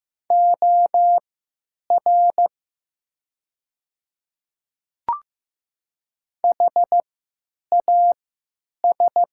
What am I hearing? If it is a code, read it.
Morse code, heard as a keyed tone, about 15 wpm, spaced as for 8 wpm, 700 Hz. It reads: OR HAS
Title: 3x Character Spacing - Top 200 Words - 15wpm